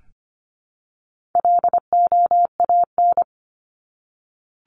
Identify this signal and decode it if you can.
Morse code, heard as a keyed tone, about 25 wpm, 700 Hz. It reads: LOAN
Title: Top 1501-1750 Words 25wpm